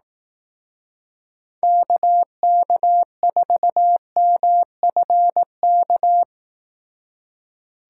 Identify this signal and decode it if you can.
Morse code, heard as a keyed tone, about 18 wpm, 700 Hz. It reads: KK4MFK
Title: Speed Racing - USA 2x3 Call Signs 15wpm